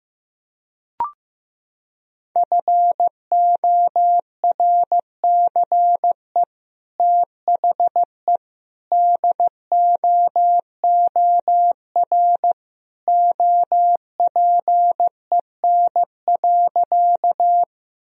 Morse code, heard as a keyed tone, about 15 wpm, 700 Hz. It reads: FORCE THE DOOR OPEN.
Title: Sentences from Top 1000 Words - Review Part 3 - 15wpm